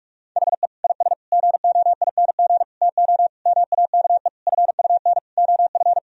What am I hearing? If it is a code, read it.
Morse code, heard as a keyed tone, about 46 wpm, 700 Hz. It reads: HE IS GOING TO MAKE FUN OF
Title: Sentences from Top 1000 Words - Review Part 4 - 45wpm